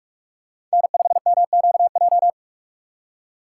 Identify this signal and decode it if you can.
Morse code, heard as a keyed tone, about 46 wpm, 700 Hz. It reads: N5MQJ
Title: USA Call Signs - 5 Characters - 4x Character Spacing - 45wpm